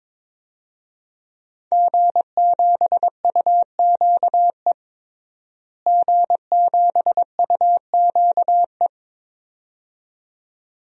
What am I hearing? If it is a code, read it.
Morse code, heard as a keyed tone, about 21 wpm, 700 Hz. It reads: G7UQE G7UQE
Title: DXCC-100 Call Signs - 5 Characters - Repeated 2x - 22wpm